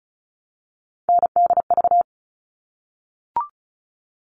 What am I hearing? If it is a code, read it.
Morse code, heard as a keyed tone, about 34 wpm, 700 Hz. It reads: NDV
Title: All Cut Numbers - 3-Digit 35wpm